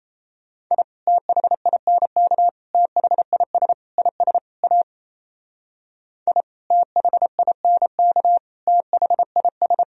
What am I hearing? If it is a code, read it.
Morse code, heard as a keyed tone, about 34 wpm, 700 Hz. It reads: I THINK THIS IS A I THINK THIS
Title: Speed-Racing - Sets of 5 Words 28wpm